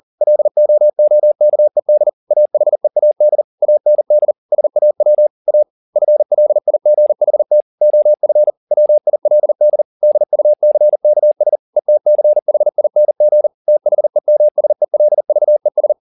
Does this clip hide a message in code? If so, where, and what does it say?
Yes: Morse code from 0.2 to 15.9 s, heard as a keyed tone, about 40 wpm, 600 Hz. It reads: LOOKED AHEAD AND SAW A FLIGHT OF WILD DUCKS ETCHING THEMSELVES